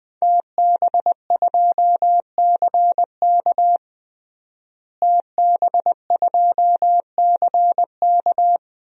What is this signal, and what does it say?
Morse code, heard as a keyed tone, about 20 wpm, 700 Hz. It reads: TB2CK TB2CK
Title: DXCC-100 Call Signs - 5 Characters - Repeated 2x - 20wpm